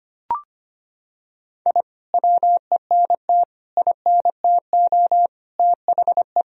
Morse code, heard as a keyed tone, about 25 wpm, 700 Hz. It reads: I WENT INTO THE
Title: Sentences from Top 1000 Words - Review Part 4 - 25wpm